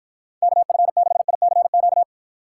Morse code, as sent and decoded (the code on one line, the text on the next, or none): -.- ..-. -.... .. -.-. --.-
KF6ICQ